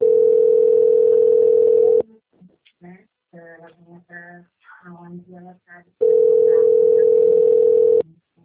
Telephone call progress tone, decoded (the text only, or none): ringback tone